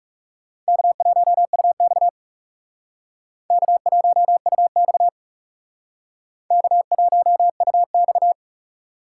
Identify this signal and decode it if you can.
Morse code, heard as a keyed tone, about 39 wpm, 700 Hz. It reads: K1UX K1UX K1UX